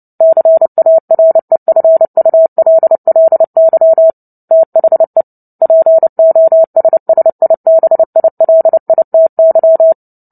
Morse code, heard as a keyed tone, about 30 wpm, 650 Hz. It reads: CAREFULLY THE POSSIBILITY
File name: Book Two / 03-Chapter 3 - 29 WPM